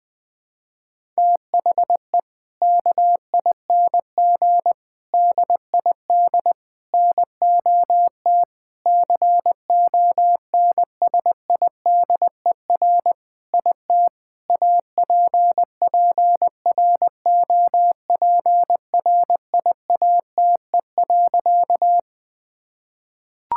Morse code, heard as a keyed tone, about 21 wpm, 700 Hz. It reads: THE KING DID NOT CONSIDER IT APPROPRIATE.
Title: Sentences from Top 1000 Words - Review Part 5 - 20wpm